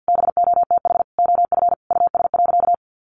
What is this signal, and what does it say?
Morse code, heard as a keyed tone, about 50 wpm, 700 Hz. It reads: BOTH OF US.
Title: Sentences from Top 1100 Words 50wpm